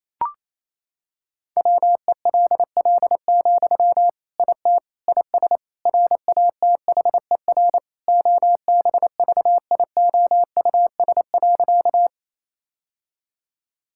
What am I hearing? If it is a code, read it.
Morse code, heard as a keyed tone, about 28 wpm, 700 Hz. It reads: WELL, IT IS RATHER OBVIOUS.
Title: Sentences from Top 1600 Words 28wpm